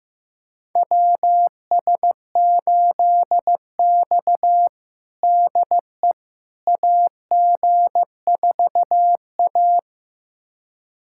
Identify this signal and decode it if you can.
Morse code, heard as a keyed tone, about 16 wpm, 700 Hz. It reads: WS8X DE AG4A
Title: USA Callsign DE Callsign - 4 Characters - 15wpm